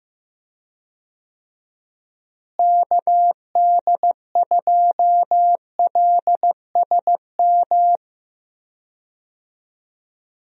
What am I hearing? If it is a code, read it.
Morse code, heard as a keyed tone, about 15 wpm, 700 Hz. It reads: KD2LSM
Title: USA Call Signs - 6 Characters - Repeated 3x - 15wpm